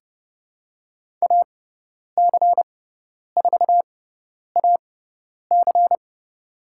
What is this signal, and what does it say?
Morse code, heard as a keyed tone, about 30 wpm, spaced as for 10 wpm, 700 Hz. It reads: AC4AC
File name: Lesson 18g - Letter m - Call Signs 30wpm